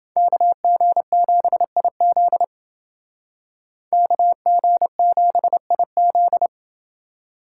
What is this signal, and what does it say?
Morse code, heard as a keyed tone, about 29 wpm, 700 Hz. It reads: KG7IZ KG7IZ